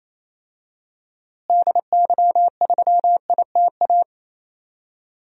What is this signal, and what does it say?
Morse code, heard as a keyed tone, about 28 wpm, 700 Hz. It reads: DY3ITA
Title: DXCC-100 Call Signs - 6 Characters - Repeated 3x - 28wpm